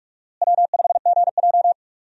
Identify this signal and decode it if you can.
Morse code, heard as a keyed tone, about 46 wpm, 700 Hz. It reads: W5GJ
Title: USA Call Signs - 4 Characters - Repeated 3x - 45wpm